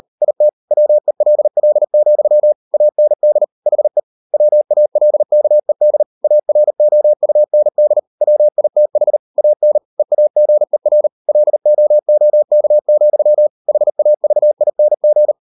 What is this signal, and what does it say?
Morse code, heard as a keyed tone, about 38 wpm, 600 Hz. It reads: IT WELL, AND HE WALKED AROUND WITH AN EAGER LOOK, HAVING